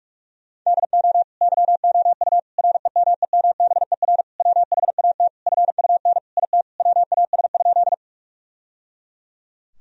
Morse code, heard as a keyed tone, about 44 wpm, 700 Hz. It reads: DO YOU REMEMBER WHAT FUN IT WAS?
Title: Sentences from Top 1000 Words - Review Part 4 - 45wpm